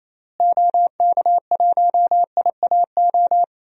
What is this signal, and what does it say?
Morse code, heard as a keyed tone, about 28 wpm, 700 Hz. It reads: OK1IAO
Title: DXCC-100 Call Signs - 6 Characters - Repeated 3x - 28wpm